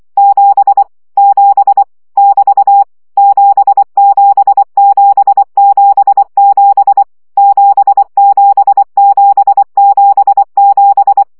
Morse code, heard as a keyed tone, about 24 wpm, 800 Hz. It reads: 7 7 <BT> 77777 77777
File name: CW gyakorlas 7 20 baud